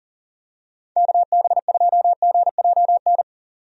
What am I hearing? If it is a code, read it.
Morse code, heard as a keyed tone, about 40 wpm, 700 Hz. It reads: KB2GJN
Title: USA Call Signs - 6 Characters - 40wpm